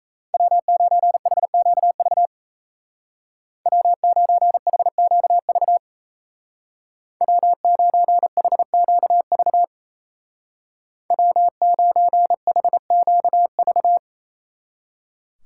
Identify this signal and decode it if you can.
Morse code, heard as a keyed tone, about 37 wpm, 700 Hz. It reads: W9HQV W9HQV W9HQV W9HQV